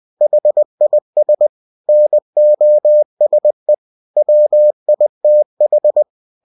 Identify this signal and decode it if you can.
Morse code, heard as a keyed tone, about 20 wpm, 600 Hz. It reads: HIS NOSE WITH
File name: cw-books / winnie-the-pooh / winnie-20wpm-page-0076